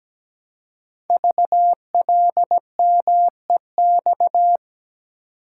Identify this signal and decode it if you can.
Morse code, heard as a keyed tone, about 17 wpm, 700 Hz. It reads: VLMEX
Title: Code Groups - 5 Letters - 17wpm